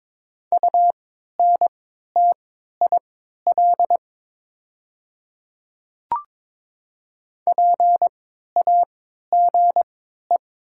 Morse code, heard as a keyed tone, about 22 wpm, spaced as for 13 wpm, 700 Hz. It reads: UNTIL PAGE
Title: 3x Character Spacing - Top 300 Words - 22wpm